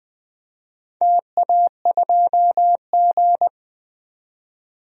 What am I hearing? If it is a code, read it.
Morse code, heard as a keyed tone, about 20 wpm, 700 Hz. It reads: TA2G
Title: DXCC-100 Call Signs - 4 Characters - 20wpm